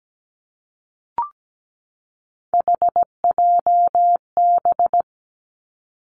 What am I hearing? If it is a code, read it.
Morse code, heard as a keyed tone, about 17 wpm, 700 Hz. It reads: HJB
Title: Three Challenging Letter-Number Combinations 17wpm